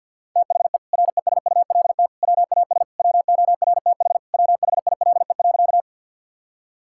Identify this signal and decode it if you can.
Morse code, heard as a keyed tone, about 52 wpm, 700 Hz. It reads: THE RESULT WAS WORTH WHILE.
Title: Sentences from Top 1000 Words - Review Part 5 - 50wpm